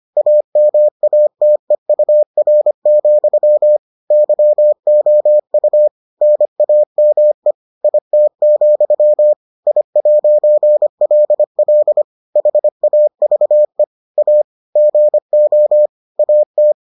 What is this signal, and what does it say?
Morse code, heard as a keyed tone, about 25 wpm, 600 Hz. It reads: AMATEUR, YOU NAME IT, I'LL HAVE A GO AT